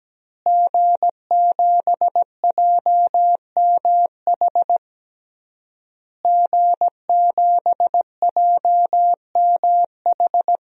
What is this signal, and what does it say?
Morse code, heard as a keyed tone, about 17 wpm, 700 Hz. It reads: G7JMH G7JMH
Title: DXCC-100 Call Signs - 5 Characters - Repeated 3x - 17wpm